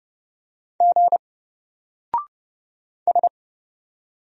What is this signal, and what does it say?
Morse code, heard as a keyed tone, about 31 wpm, 700 Hz. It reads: G S
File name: Single-Letter-Number-30wpm